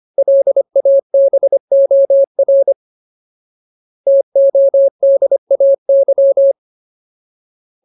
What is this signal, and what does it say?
Morse code, heard as a keyed tone, about 25 wpm, 550 Hz. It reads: LABOR TODAY
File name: Ultimate Word List / 25WPM Ultimate